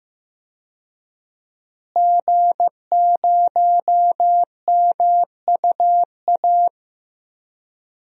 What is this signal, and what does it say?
Morse code, heard as a keyed tone, about 15 wpm, 700 Hz. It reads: G0MUA
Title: DXCC-100 Call Signs - 5 Characters - Repeated 3x - 15wpm